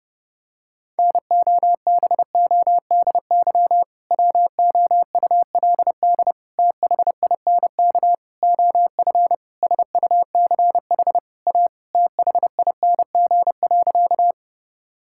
Morse code, heard as a keyed tone, about 32 wpm, 700 Hz. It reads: NOBODY WOULD THINK OF SUCH A THING.
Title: Sentences from Top 1600 Words 30wpm